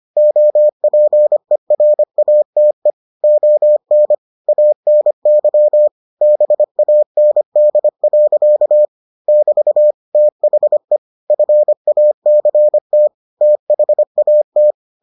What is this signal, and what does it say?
Morse code, heard as a keyed tone, about 25 wpm, 600 Hz. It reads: OPERATE ON ANY BAND. <BT> THE FACT THAT